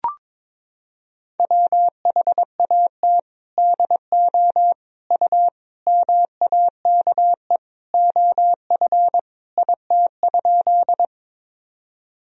Morse code, heard as a keyed tone, about 22 wpm, 700 Hz. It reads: WHAT DO U MAKE OF IT?
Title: Sentences from Top 100 Words and Abbreviations 22wpm